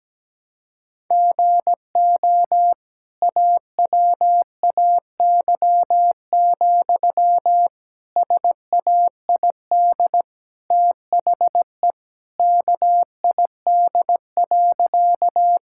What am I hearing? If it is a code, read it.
Morse code, heard as a keyed tone, about 17 wpm, 700 Hz. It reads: GO AWAY, SAID THE KID.
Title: Sentences from Top 1600 Words 17wpm